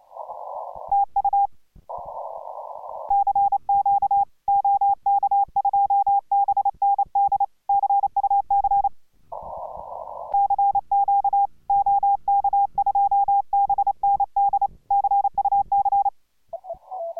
Morse code, heard as a keyed tone, about 29 wpm, 800 Hz. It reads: TU CQ OK2BND CUC CQ OK2BND CUC